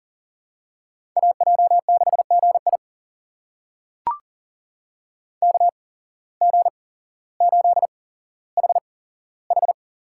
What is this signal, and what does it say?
Morse code, heard as a keyed tone, about 40 wpm, spaced as for 17 wpm, 700 Hz. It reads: AJ6GI KG8HH
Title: USA Call Signs - 5 Characters - 8x Character Spacing - 40wpm